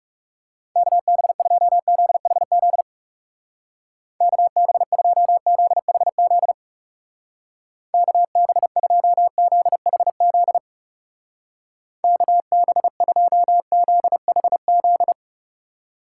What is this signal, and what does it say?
Morse code, heard as a keyed tone, about 37 wpm, 700 Hz. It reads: KB2ZHZ KB2ZHZ KB2ZHZ KB2ZHZ
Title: Speed Racing - USA Call Signs 30wpm